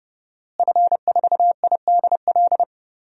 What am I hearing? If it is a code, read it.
Morse code, heard as a keyed tone, about 32 wpm, 700 Hz. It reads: F4IDL
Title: DXCC-100 Call Signs - 5 Characters - Repeated 2x - 30wpm